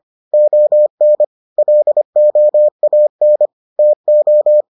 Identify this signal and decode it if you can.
Morse code, heard as a keyed tone, about 25 wpm, 600 Hz. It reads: ON LOAN TO